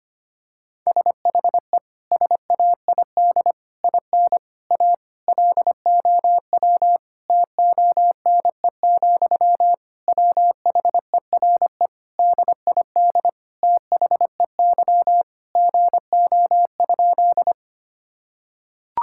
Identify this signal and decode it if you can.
Morse code, heard as a keyed tone, about 25 wpm, 700 Hz. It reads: SHE SAID IN A LOW TONE, WHERE DID THEY GO?
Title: Sentences from Top 1600 Words 25wpm